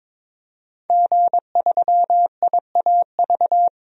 Morse code, heard as a keyed tone, about 22 wpm, 700 Hz. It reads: G3IAV
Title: DXCC-100 Call Signs - 5 Characters - 22wpm